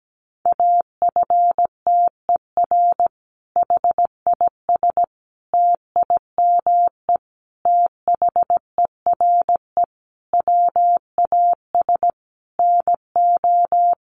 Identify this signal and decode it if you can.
Morse code, heard as a keyed tone, about 17 wpm, 700 Hz. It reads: AFTER HIS TIME THERE WAS NO
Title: Sentences from Top 1400 Words 17wpm